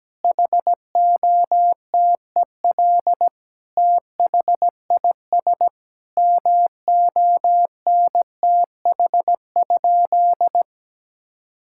Morse code, heard as a keyed tone, about 17 wpm, 700 Hz. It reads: HOTEL THIS MONTH?
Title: Sentences from Top 1600 Words 17wpm